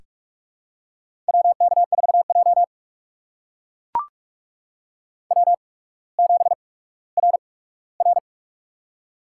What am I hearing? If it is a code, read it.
Morse code, heard as a keyed tone, about 44 wpm, spaced as for 19 wpm, 700 Hz. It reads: WK4J W7RR
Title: USA Call Signs - 4 Characters - 8x Character Spacing - 45wpm